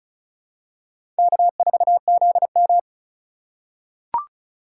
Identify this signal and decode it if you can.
Morse code, heard as a keyed tone, about 34 wpm, 700 Hz. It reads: K4ZM
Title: USA Call Signs - 6x Character Spacing - 35wpm